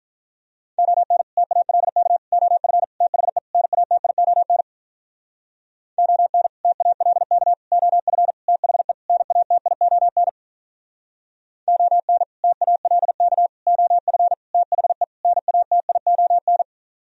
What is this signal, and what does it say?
Morse code, heard as a keyed tone, about 46 wpm, 700 Hz. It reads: ON TALK OF THE NATION ON TALK OF THE NATION ON TALK OF THE NATION